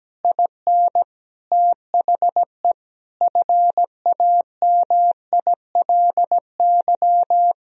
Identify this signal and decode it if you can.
Morse code, heard as a keyed tone, about 17 wpm, 700 Hz. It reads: IN THE FAMILY
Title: Sentences from Top 1000 Words - Review Part 3 - 17wpm